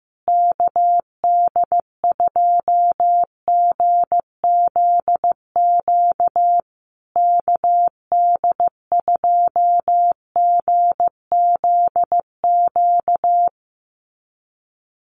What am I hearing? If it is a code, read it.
Morse code, heard as a keyed tone, about 15 wpm, 700 Hz. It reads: KD2GZQ KD2GZQ
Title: USA Call Signs Repeated 3x 15wpm